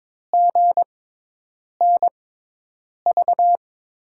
Morse code, heard as a keyed tone, about 22 wpm, spaced as for 8 wpm, 700 Hz. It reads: GNV
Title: USA Call Signs - 6 Characters - 6x Character Spacing - 22wpm